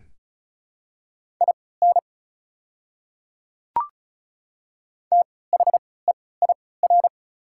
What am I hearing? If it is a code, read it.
Morse code, heard as a keyed tone, about 36 wpm, spaced as for 20 wpm, 690 Hz. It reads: IN THEIR
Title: 3x Character Spacing - Top 100 Words - 35wpm